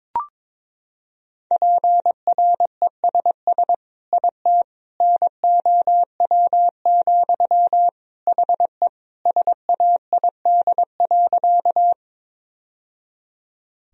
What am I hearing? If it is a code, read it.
Morse code, heard as a keyed tone, about 22 wpm, 700 Hz. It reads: PRESS IT NOW, HE SAID.
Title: Sentences from Top 1000 Words - Review Part 1 - 22wpm